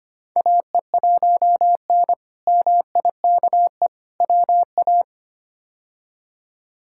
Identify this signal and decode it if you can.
Morse code, heard as a keyed tone, about 25 wpm, 700 Hz. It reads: AE1N MIKE WA
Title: K1USN SST QSO Practice - 25wpm